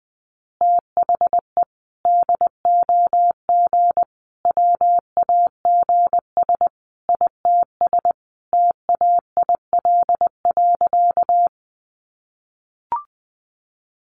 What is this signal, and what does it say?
Morse code, heard as a keyed tone, about 20 wpm, 700 Hz. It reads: THE DOG WAGS ITS TAIL.
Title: Easy Phrases 20wpm